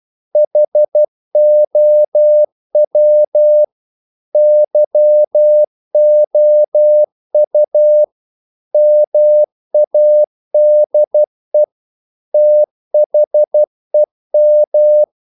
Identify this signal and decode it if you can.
Morse code, heard as a keyed tone, about 12 wpm, 600 Hz. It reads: HOW YOU MADE THEM
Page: English quotations in CW (telegraphy - morse)